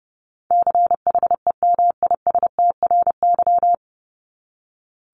Morse code, heard as a keyed tone, about 30 wpm, 700 Hz. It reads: CHEMISTRY
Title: Top 751-1000 Words 30wpm